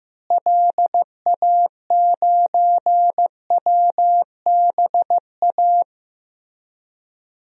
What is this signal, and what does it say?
Morse code, heard as a keyed tone, about 14 wpm, 700 Hz. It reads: LA9WBA
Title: DXCC-100 Call Signs - 6 Characters - Repeated 3x - 15wpm